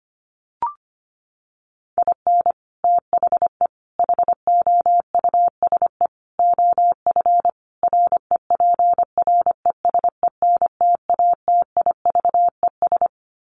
Morse code, heard as a keyed tone, about 25 wpm, 700 Hz. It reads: IN THE HOUSE OF REPRESENTATIVES
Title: Sets of 5 Words Encore 25wpm